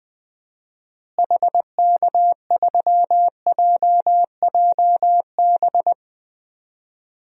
Morse code, heard as a keyed tone, about 20 wpm, 700 Hz. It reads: HK3JJB
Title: DXCC-100 Call Signs - 6 Characters - Repeated 2x - 20wpm